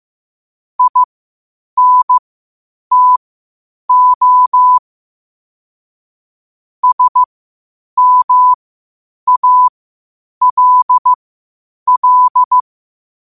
Morse code, heard as a keyed tone, about 15 wpm, spaced as for 8 wpm, 1000 Hz. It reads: INTO SMALL
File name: qotdfiles / 8wpm